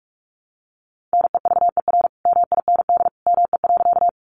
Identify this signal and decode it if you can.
Morse code, heard as a keyed tone, about 46 wpm, 700 Hz. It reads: NEVER MIND ME.
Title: Sentences from Top 500 Words 45wpm